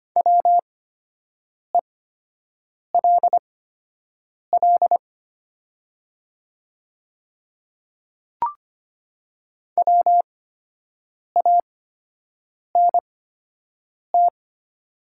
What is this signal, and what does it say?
Morse code, heard as a keyed tone, about 25 wpm, spaced as for 7 wpm, 700 Hz. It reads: WELL WANT
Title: 8x Character Spacing - Top 200 Words - 25wpm